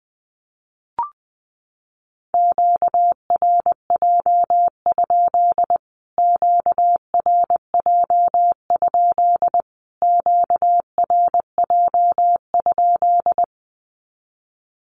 Morse code, heard as a keyed tone, about 20 wpm, 700 Hz. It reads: QRJ? QRJ? QRJ?
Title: Q-Codes Repeated 3x 20wpm